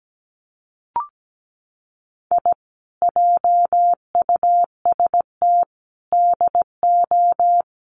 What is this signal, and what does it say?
Morse code, heard as a keyed tone, about 17 wpm, 700 Hz. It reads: I JUST DO
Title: Sets of 5 Words Encore 17wpm